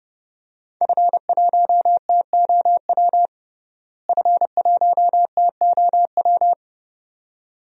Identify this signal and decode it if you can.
Morse code, heard as a keyed tone, about 30 wpm, 700 Hz. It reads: F1TOW F1TOW
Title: DXCC-100 Call Signs - 5 Characters - Repeated 2x - 30wpm